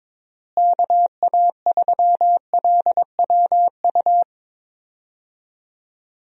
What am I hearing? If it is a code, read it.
Morse code, heard as a keyed tone, about 22 wpm, 700 Hz. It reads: KA3LWU
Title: USA Call Signs - 6 Characters - Repeated 2x - 22wpm